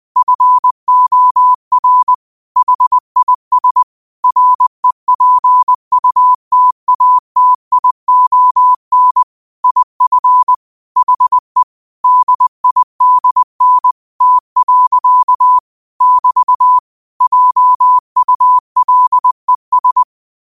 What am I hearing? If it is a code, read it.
Morse code, heard as a keyed tone, about 20 wpm, 1000 Hz. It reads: FOR HIS REPUTATION IF HE DIDN T. <BT> JULES